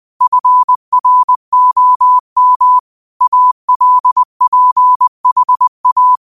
Morse code, heard as a keyed tone, about 20 wpm, 1000 Hz. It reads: FROM ALPHA
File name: qotdfiles / 20wpm